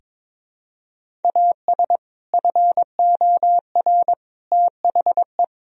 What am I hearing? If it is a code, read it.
Morse code, heard as a keyed tone, about 21 wpm, 700 Hz. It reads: AS FOR THE
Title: Sentences from Top 1000 Words - Review Part 1 - 22wpm